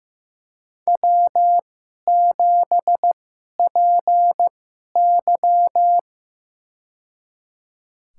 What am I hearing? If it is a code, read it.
Morse code, heard as a keyed tone, about 15 wpm, spaced as for 11 wpm, 700 Hz. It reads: W7PY
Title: USA Call Signs - 4 Characters - 2x Character Spacing - 15wpm